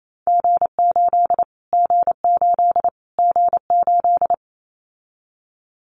Morse code, heard as a keyed tone, about 28 wpm, 700 Hz. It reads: G8 G8 G8
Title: Callsign Prefixes for Top 25 DXCC - Repeated 3x - 28wpm